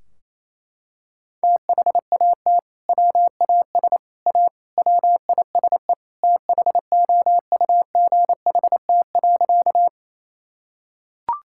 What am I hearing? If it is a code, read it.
Morse code, heard as a keyed tone, about 28 wpm, 700 Hz. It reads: THAT WAS A WISE THOUGHT.
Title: Sentences from Top 1600 Words 28wpm